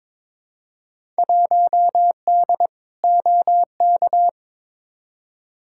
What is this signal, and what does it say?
Morse code, heard as a keyed tone, about 22 wpm, 700 Hz. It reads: 1D OK
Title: Field Day Exchanges - 22wpm